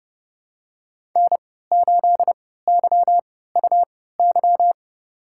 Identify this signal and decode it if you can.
Morse code, heard as a keyed tone, about 30 wpm, spaced as for 17 wpm, 700 Hz. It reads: N8YUY
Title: USA Call Signs - 5 Characters - 3x Character Spacing - 30wpm